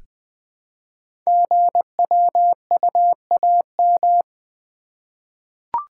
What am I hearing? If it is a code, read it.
Morse code, heard as a keyed tone, about 20 wpm, 700 Hz. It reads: GWUAM
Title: Code Groups - 5 Letters - 20wpm